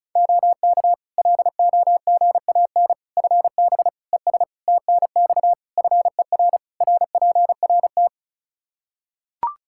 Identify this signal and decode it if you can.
Morse code, heard as a keyed tone, about 36 wpm, 700 Hz. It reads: OK LOGAN FB ES TNX FER RPRT
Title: Practice QSO - Exchange 2 - Info - 35wpm